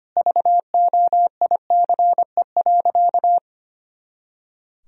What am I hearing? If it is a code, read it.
Morse code, heard as a keyed tone, about 25 wpm, 700 Hz. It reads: VOICE.
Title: Sentences from Top 500 Words and Abbreviations 25wpm